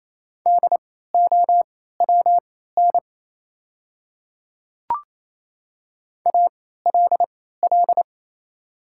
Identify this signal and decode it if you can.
Morse code, heard as a keyed tone, about 27 wpm, spaced as for 16 wpm, 690 Hz. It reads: DOWN ALL
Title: 3x Character Spacing - Top 100 Words - 28wpm